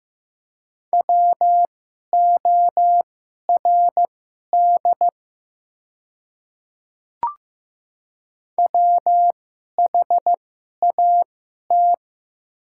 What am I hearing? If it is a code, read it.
Morse code, heard as a keyed tone, about 15 wpm, spaced as for 11 wpm, 690 Hz. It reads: WORD WHAT
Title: 2x Character Spacing - Top 300 Words - 15wpm